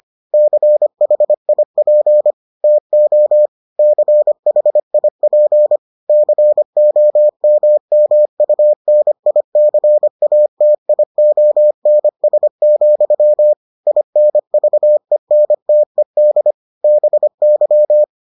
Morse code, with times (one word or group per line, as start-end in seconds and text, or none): 0.3-2.3 CHIP
2.6-3.5 TO
3.8-5.8 CHIP
6.1-13.5 COMMUNICATIONS,
13.9-16.5 INVENTED
16.8-18.0 BY